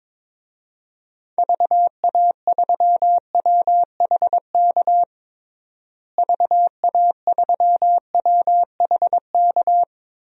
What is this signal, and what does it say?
Morse code, heard as a keyed tone, about 22 wpm, 700 Hz. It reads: VA3WHK VA3WHK
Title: DXCC-100 Call Signs - 6 Characters - Repeated 2x - 22wpm